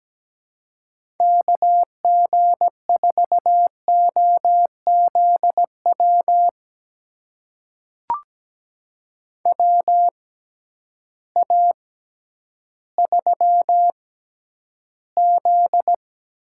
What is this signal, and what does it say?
Morse code, heard as a keyed tone, about 17 wpm, spaced as for 10 wpm, 700 Hz. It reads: KG4OZW WA3Z
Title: USA Call Signs - 6x Character Spacing - 17wpm